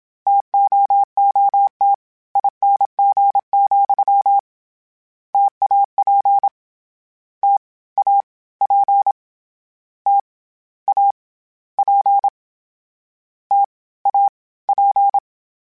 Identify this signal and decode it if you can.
Morse code, heard as a keyed tone, about 27 wpm, 800 Hz. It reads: TOOT ING, TAP T A P T A P T A P
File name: Variable Spacing / Variable Spacing-mp3s / T AOEs @25